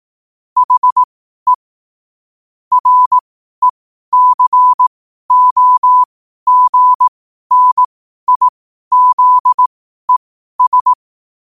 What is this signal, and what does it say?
Morse code, heard as a keyed tone, about 18 wpm, spaced as for 12 wpm, 1000 Hz. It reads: HE RECOGNIZES
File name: qotdfiles / 12wpm